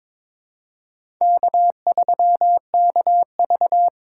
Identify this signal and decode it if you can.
Morse code, heard as a keyed tone, about 22 wpm, 700 Hz. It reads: K3KV
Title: USA Call Signs - 4 Characters - 2x Character Spacing - 22wpm